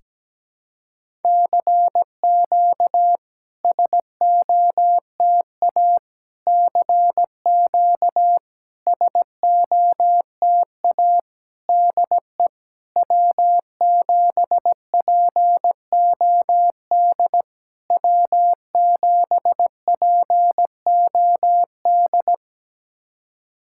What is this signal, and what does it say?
Morse code, heard as a keyed tone, about 17 wpm, 700 Hz. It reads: CQ SOTA CQ SOTA DE W7POD W7POD
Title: SOTA QSO Practice 17wpm